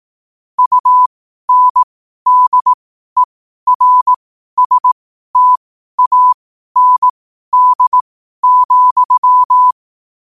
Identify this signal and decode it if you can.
Morse code, heard as a keyed tone, about 18 wpm, spaced as for 12 wpm, 1000 Hz. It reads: UNDERSTAND,
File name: qotdfiles / 12wpm